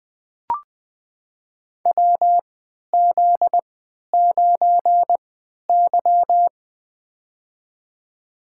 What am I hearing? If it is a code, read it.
Morse code, heard as a keyed tone, about 20 wpm, spaced as for 11 wpm, 700 Hz. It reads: WZ9Y
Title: USA Call Signs - 4 Characters - 3x Character Spacing - 20wpm